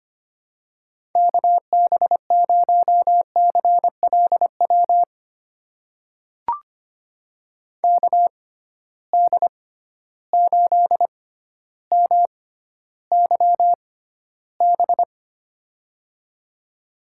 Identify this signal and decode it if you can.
Morse code, heard as a keyed tone, about 25 wpm, spaced as for 13 wpm, 700 Hz. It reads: KB0CLW KD8MYB
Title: USA Call Signs - 6 Characters - 6x Character Spacing - 25wpm